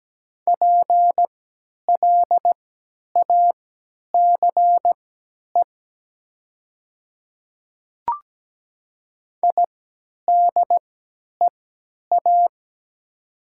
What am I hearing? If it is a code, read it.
Morse code, heard as a keyed tone, about 17 wpm, spaced as for 10 wpm, 700 Hz. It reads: PLACE IDEA
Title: 3x Character Spacing - Top 500 Words - 17wpm